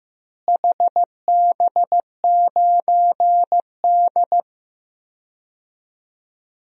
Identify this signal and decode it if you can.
Morse code, heard as a keyed tone, about 15 wpm, 700 Hz. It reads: HB9D
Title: DXCC-100 Call Signs - 4 Characters - Repeated 2x - 15wpm